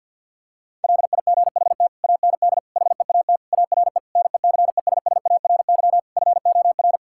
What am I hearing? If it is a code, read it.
Morse code, heard as a keyed tone, about 50 wpm, 700 Hz. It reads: LIGHT AND HEAT ARE NECESSARY FOR
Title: Sentences from Top 1000 Words - Review Part 5 - 50wpm